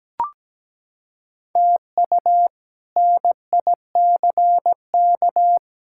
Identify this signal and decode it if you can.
Morse code, heard as a keyed tone, about 17 wpm, 700 Hz. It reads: TU NICK
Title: K1USN SST QSO Practice - 17wpm